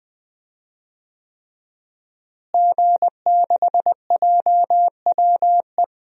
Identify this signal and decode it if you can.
Morse code, heard as a keyed tone, about 20 wpm, 700 Hz. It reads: G6JWE
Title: DXCC-100 Call Signs - 5 Characters - Repeated 3x - 20wpm